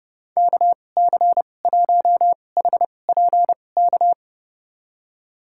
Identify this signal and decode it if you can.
Morse code, heard as a keyed tone, about 32 wpm, spaced as for 22 wpm, 700 Hz. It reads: KC1HPK
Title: USA Call Signs - 6 Characters - 2x Character Spacing - 30wpm